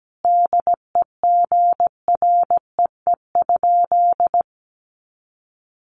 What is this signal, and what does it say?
Morse code, heard as a keyed tone, about 17 wpm, 700 Hz. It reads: DEGREE?
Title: Sentences from Top 1500 Words 17wpm